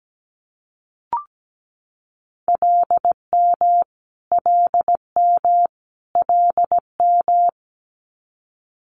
Morse code, heard as a keyed tone, about 17 wpm, 700 Hz. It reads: LM LM LM
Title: Multi-letter Phonemes Repeated 3x 17wpm